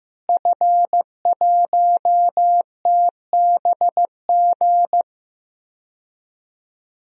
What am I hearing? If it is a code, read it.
Morse code, heard as a keyed tone, about 15 wpm, 700 Hz. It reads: F1TBG
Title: DXCC-100 Call Signs - 5 Characters - Repeated 3x - 15wpm